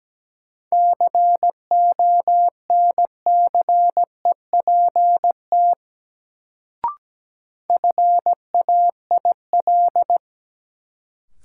Morse code, heard as a keyed tone, about 17 wpm, 700 Hz. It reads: CONCEPT FAIL
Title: Sentences from Top 1100 Words - New Word Drill - 17wpm